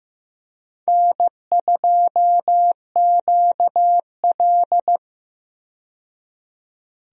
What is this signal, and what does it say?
Morse code, heard as a keyed tone, about 15 wpm, 700 Hz. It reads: N2QL
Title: USA Call Signs - 4 Characters - 15wpm